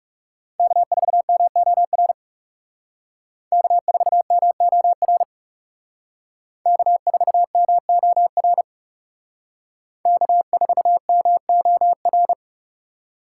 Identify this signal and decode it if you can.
Morse code, heard as a keyed tone, about 35 wpm, 700 Hz. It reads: K4MOR K4MOR K4MOR K4MOR
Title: Speed Racing - USA Call Signs 30wpm